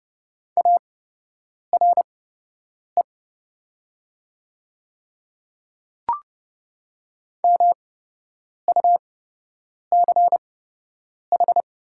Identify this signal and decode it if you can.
Morse code, heard as a keyed tone, about 29 wpm, spaced as for 8 wpm, 700 Hz. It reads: ARE MUCH
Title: 8x Character Spacing - Top 200 Words - 30wpm